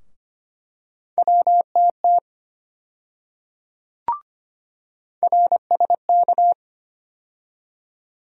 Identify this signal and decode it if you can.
Morse code, heard as a keyed tone, about 25 wpm, 700 Hz. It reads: WTT RSK
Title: USA Call Sign Postfixes - 3 Letters 25wpm